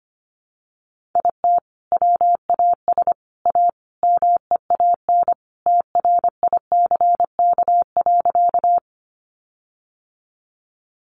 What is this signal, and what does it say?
Morse code, heard as a keyed tone, about 25 wpm, 700 Hz. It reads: IT WAS A MEAN TRICK.
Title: Sentences from Top 1500 Words 25wpm